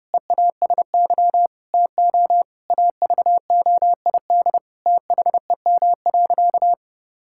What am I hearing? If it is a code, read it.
Morse code, heard as a keyed tone, about 32 wpm, 700 Hz. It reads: EASY TO AVOID THEM.
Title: Sentences from Top 1000 Words - Review Part 6 - 30wpm